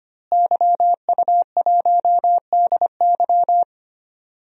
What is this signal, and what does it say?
Morse code, heard as a keyed tone, about 25 wpm, 700 Hz. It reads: YU1DY
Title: DXCC-100 Call Signs - 5 Characters - Repeated 3x - 25wpm